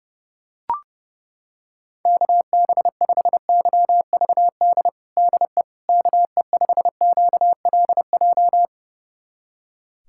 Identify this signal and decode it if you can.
Morse code, heard as a keyed tone, about 30 wpm, 700 Hz. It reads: KB5YVD DE KE5QLJ
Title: USA Callsign DE Callsign - 6 Characters - 30wpm